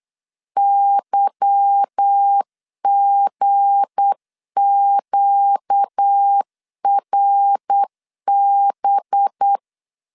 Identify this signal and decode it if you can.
Morse code, heard as a keyed tone, about 8 wpm, 800 Hz. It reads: YGQRB
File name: LEZIONI / 8 YJ45 / 8 Y 8wpm 3245